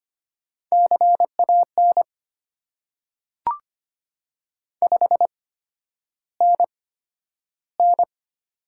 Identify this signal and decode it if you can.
Morse code, heard as a keyed tone, about 25 wpm, spaced as for 11 wpm, 700 Hz. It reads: CAN 5NN
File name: Top 90 QSO Elements - 8x Character Spacing - 25wpm